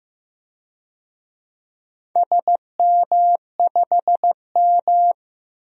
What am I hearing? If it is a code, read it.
Morse code, heard as a keyed tone, about 15 wpm, 700 Hz. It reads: SM5M
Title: DXCC-100 Call Signs - 4 Characters - Repeated 3x - 15wpm